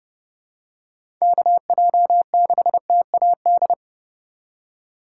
Morse code, heard as a keyed tone, about 32 wpm, 700 Hz. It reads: KJ6TAD
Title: USA Call Signs - 6 Characters - 4x Character Spacing - 30wpm